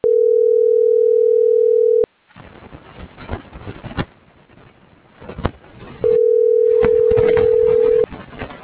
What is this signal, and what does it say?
Ringback tone (North America), 440 + 480 Hz.